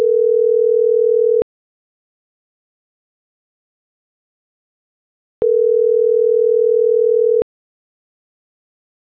Ringback tone (North America), 440 + 480 Hz.